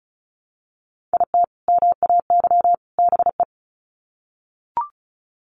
Morse code, heard as a keyed tone, about 34 wpm, 700 Hz. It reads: IT MAY BE
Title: Sets of 3 Words 35wpm